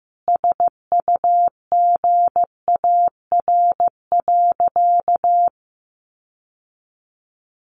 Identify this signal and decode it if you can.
Morse code, heard as a keyed tone, about 15 wpm, 700 Hz. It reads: SUGAR.
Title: Sentences from Top 1200 Words 15wpm